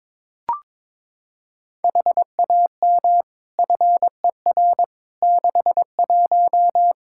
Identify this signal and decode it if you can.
Morse code, heard as a keyed tone, about 22 wpm, 700 Hz. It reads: HAM FER 61
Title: K1USN SST QSO Practice 22wpm